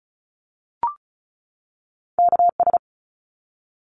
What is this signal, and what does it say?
Morse code, heard as a keyed tone, about 34 wpm, 700 Hz. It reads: KS
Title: ARRL Sections 35wpm